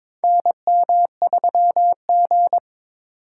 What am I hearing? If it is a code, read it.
Morse code, heard as a keyed tone, about 22 wpm, 700 Hz. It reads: NM3G
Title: USA Call Signs - 4 Characters - 6x Character Spacing - 22wpm